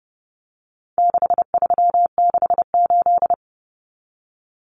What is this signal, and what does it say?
Morse code, heard as a keyed tone, about 30 wpm, 700 Hz. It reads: 6368
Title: Common Cut Numbers - 4-Digit 30wpm